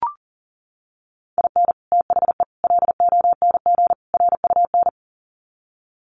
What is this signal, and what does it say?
Morse code, heard as a keyed tone, about 40 wpm, 700 Hz. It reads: IN THE LONG RUN
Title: Sets of 4 Words 40wpm